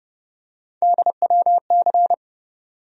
Morse code, heard as a keyed tone, about 30 wpm, 700 Hz. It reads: DWC
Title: USA Call Sign Postfixes - 3 Letters 30wpm